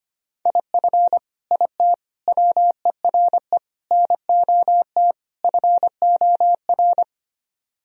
Morse code, heard as a keyed tone, about 29 wpm, 700 Hz. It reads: IF IT WERE NOT FOR
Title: Speed-Racing - Sets of 5 Words 25wpm